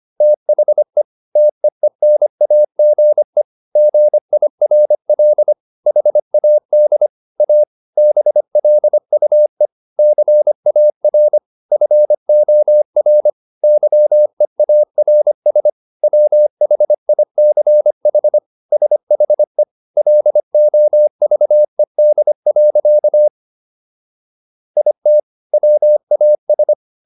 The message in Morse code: THE TEENAGE GIRL HAD A BLUE CAR FOR YEARS WHICH SHE LOVED. IT WAS

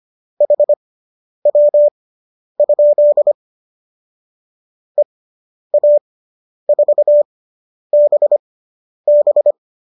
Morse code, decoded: HW? EA4BB

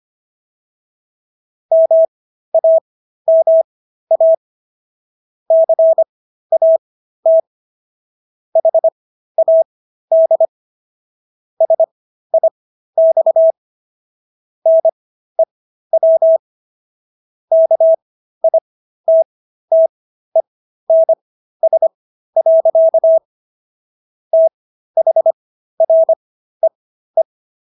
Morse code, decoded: MAMA CAT HAD SIX NEW KITTENS. THREE

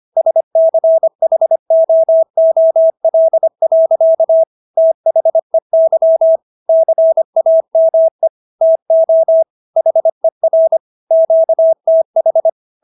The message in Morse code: SCHOOL. THEY CAME TO HER QTH

